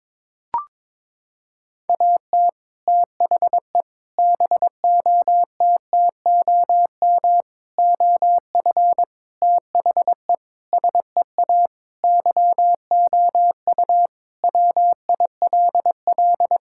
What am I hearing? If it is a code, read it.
Morse code, heard as a keyed tone, about 22 wpm, 700 Hz. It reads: AT THE BOTTOM OF THE SEA YOU WILL
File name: practice / 22wpm / Sentences with 3-Word Phrases - 22wpm